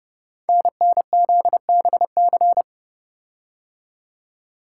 Morse code, heard as a keyed tone, about 29 wpm, 700 Hz. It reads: NNZBC
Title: Code Groups - 5 Letters - 30wpm